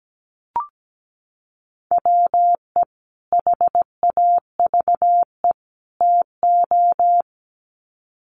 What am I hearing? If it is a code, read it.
Morse code, heard as a keyed tone, about 17 wpm, 700 Hz. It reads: WE HAVE TO
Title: Sets of 3 Words 17wpm